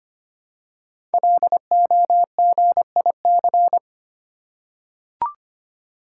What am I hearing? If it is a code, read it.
Morse code, heard as a keyed tone, about 25 wpm, 700 Hz. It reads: LOGIC
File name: 5 Letter Words - First 3 Letters Given - 25wpm